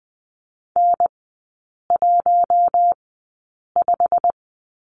Morse code, heard as a keyed tone, about 20 wpm, spaced as for 8 wpm, 700 Hz. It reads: N15
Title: 093 Lesson 16e - Letter c - Three Characters 20wpm